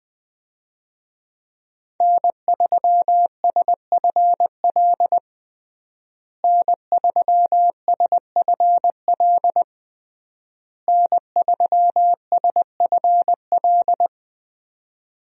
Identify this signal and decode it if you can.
Morse code, heard as a keyed tone, about 21 wpm, 700 Hz. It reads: N3SFL N3SFL N3SFL
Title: USA Call Signs - 5 Characters - Repeated 3x - 20wpm